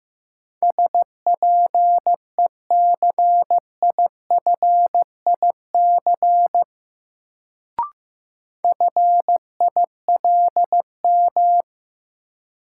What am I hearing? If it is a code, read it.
Morse code, heard as a keyed tone, about 16 wpm, 700 Hz. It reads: SPECIFIC FILM
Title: Sentences from Top 1000 Words - Review Part 3 - New Word Drill - 15wpm